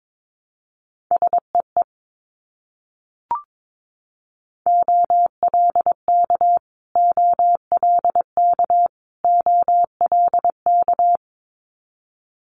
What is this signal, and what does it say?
Morse code, heard as a keyed tone, about 22 wpm, 700 Hz. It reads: SEE OLK OLK OLK
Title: Multi-letter Phonemes Repeated 3x 22wpm